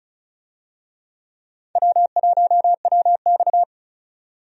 Morse code, heard as a keyed tone, about 34 wpm, 700 Hz. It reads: W1WX
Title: USA Call Signs - 4 Characters - 35wpm